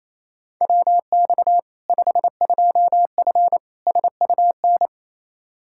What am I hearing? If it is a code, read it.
Morse code, heard as a keyed tone, about 28 wpm, 700 Hz. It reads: WX 52F SUN
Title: Practice QSO - Exchange 2 - Info - 28wpm